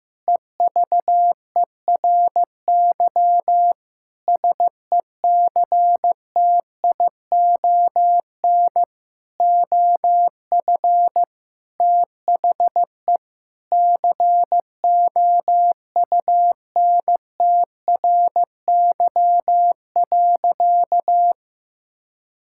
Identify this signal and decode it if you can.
Morse code, heard as a keyed tone, about 15 wpm, 700 Hz. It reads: EVERY SECTION OF THE COUNTRY.
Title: Sentences from Top 1000 Words - Review Part 5 - 15wpm